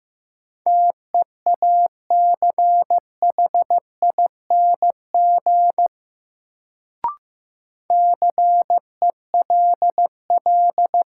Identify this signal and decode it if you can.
Morse code, heard as a keyed tone, about 15 wpm, 700 Hz. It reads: TEACHING CELL
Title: Sentences from Top 1100 Words - New Word Drill Alt - 15wpm